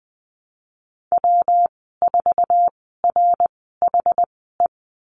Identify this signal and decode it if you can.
Morse code, heard as a keyed tone, about 20 wpm, spaced as for 14 wpm, 700 Hz. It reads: W4RHE
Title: 200 Lesson 32g - Letter k - Call Signs 20wpm